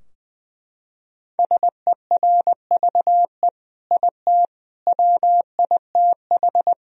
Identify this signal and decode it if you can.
Morse code, heard as a keyed tone, about 21 wpm, 700 Hz. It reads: SERVE IT WITH
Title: Sentences from Top 1000 Words - Review Part 3 - 20wpm